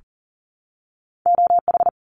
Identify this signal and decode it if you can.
Morse code, heard as a keyed tone, about 41 wpm, 700 Hz. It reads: OH
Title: US States 40wpm